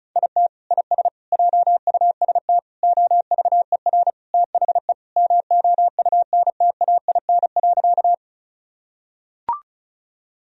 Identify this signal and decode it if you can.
Morse code, heard as a keyed tone, about 36 wpm, 700 Hz. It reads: IT IS JUST OVER THE MOUNTAIN.
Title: Sentences from Top 1000 Words - Review Part 1 - 35wpm